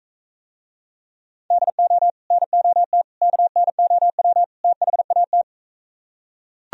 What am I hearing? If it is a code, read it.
Morse code, heard as a keyed tone, about 41 wpm, 700 Hz. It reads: DO NOT KNOW THAT